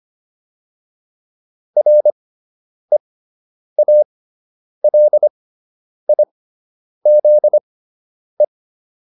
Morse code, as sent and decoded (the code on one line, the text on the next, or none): .-. .- .-.. .. --.. .
REALIZE